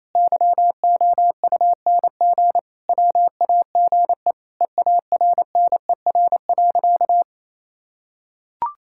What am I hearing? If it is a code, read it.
Morse code, heard as a keyed tone, about 28 wpm, 700 Hz. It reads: YOUNG WAGE EARNER.
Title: Pangram Challenge v2 - 28wpm